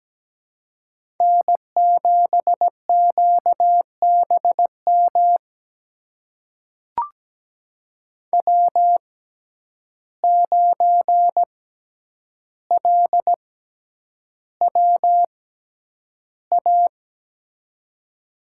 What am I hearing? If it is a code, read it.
Morse code, heard as a keyed tone, about 17 wpm, spaced as for 9 wpm, 700 Hz. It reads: N7QBM W9LWA
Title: USA Call Signs - 5 Characters - 6x Character Spacing - 17wpm